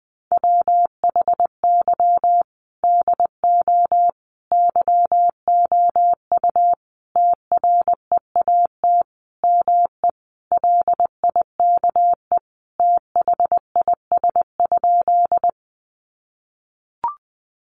Morse code, heard as a keyed tone, about 20 wpm, 700 Hz. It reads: WHY DO YOU TREAT ME LIKE THIS?
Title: Sentences from Top 1300 Words 20wpm